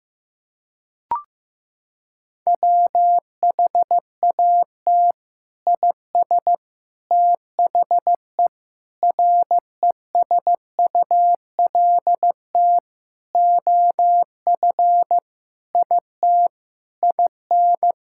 Morse code, heard as a keyed tone, about 15 wpm, 700 Hz. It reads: WHAT IS THE RESULT OF IT IN
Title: Sentences from Top 1000 Words - Review Part 5 - 15wpm